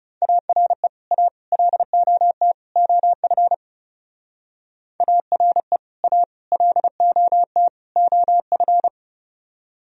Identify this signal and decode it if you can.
Morse code, heard as a keyed tone, about 36 wpm, 700 Hz. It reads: ARE A LOT OF ARE A LOT OF
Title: Speed-Racing - Sets of 4 Words 30wpm